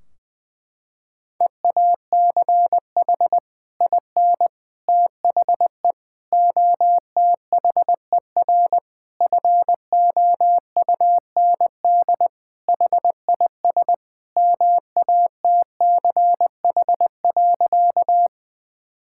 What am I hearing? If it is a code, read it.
Morse code, heard as a keyed tone, about 21 wpm, 700 Hz. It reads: EACH IN THE OTHER FOUND HIS MATCH.
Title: Sentences from Top 1600 Words 20wpm